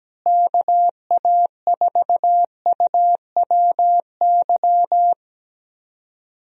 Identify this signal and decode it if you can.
Morse code, heard as a keyed tone, about 17 wpm, 700 Hz. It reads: KA4UWY
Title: USA Call Signs - 6 Characters - Repeated 3x - 17wpm